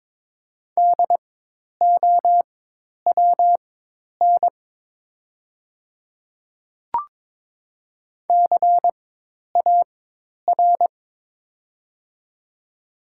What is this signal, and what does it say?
Morse code, heard as a keyed tone, about 22 wpm, spaced as for 10 wpm, 700 Hz. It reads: DOWN CAR
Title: 4x Character Spacing - Top 300 Words - 22wpm